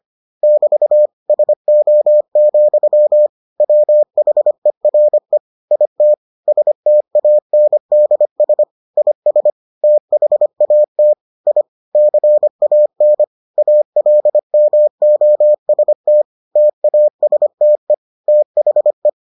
Morse code, heard as a keyed tone, about 25 wpm, 600 Hz. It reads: <BT> SO, WHERE IT STANDS IS THAT I CAN ALMOST TASTE THE